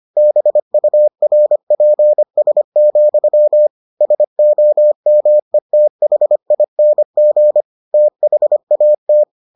Morse code, heard as a keyed tone, about 25 wpm, 600 Hz. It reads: BURPS, SOMETHING THAT